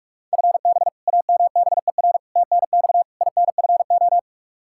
Morse code, heard as a keyed tone, about 46 wpm, 700 Hz. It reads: FB AMBER TNX INFO
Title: Practice QSO - Exchange 3 - More Info - 45wpm